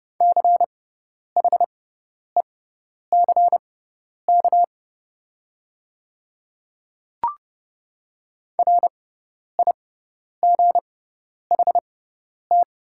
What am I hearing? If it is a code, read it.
Morse code, heard as a keyed tone, about 30 wpm, spaced as for 10 wpm, 700 Hz. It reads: CHECK RIGHT
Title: 6x Character Spacing - Top 500 Words - 30wpm